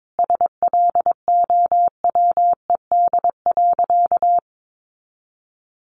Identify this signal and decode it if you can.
Morse code, heard as a keyed tone, about 22 wpm, 700 Hz. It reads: SLOWED.
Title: A Sound Of Thunder 22wpm